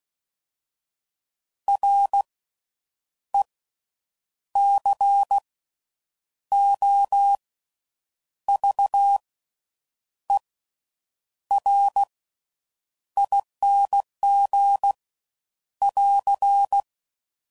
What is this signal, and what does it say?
Morse code, heard as a keyed tone, about 16 wpm, spaced as for 7 wpm, 800 Hz. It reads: RECOVERING <AR>
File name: Slow With Faster Ings